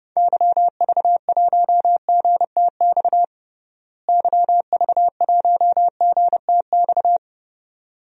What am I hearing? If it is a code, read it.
Morse code, heard as a keyed tone, about 30 wpm, 700 Hz. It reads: YV1GTX YV1GTX
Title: DXCC-100 Call Signs - 6 Characters - Repeated 2x - 30wpm